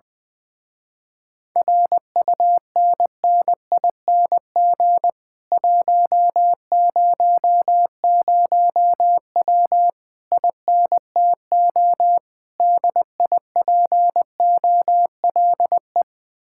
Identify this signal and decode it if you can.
Morse code, heard as a keyed tone, about 20 wpm, 700 Hz. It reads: RUNNING 100W INTO DIPOLE
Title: Speed-Racing Common QSO Phrases - 20wpm